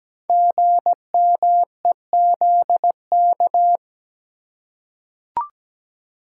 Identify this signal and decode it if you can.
Morse code, heard as a keyed tone, about 17 wpm, 700 Hz. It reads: GMEZK
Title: Code Groups - 5 Letter:Numbers - 17wpm